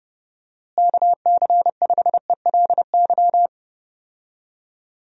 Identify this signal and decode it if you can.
Morse code, heard as a keyed tone, about 30 wpm, 700 Hz. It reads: KC5ELY
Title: USA Call Signs - 6 Characters - Repeated 2x - 30wpm